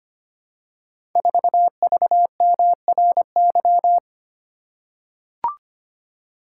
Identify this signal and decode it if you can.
Morse code, heard as a keyed tone, about 25 wpm, 700 Hz. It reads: 4VMRY
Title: Code Groups - 5 Letter:Numbers - 25wpm